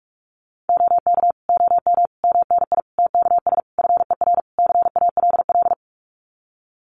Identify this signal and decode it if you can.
Morse code, heard as a keyed tone, about 44 wpm, 700 Hz. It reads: OK OM MNI TKS FER CALL
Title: Common QSO Phrases 45wpm